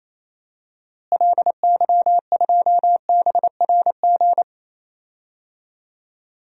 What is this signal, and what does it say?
Morse code, heard as a keyed tone, about 28 wpm, 700 Hz. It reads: LY2BRG
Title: DXCC-100 Call Signs - 6 Characters - Repeated 3x - 28wpm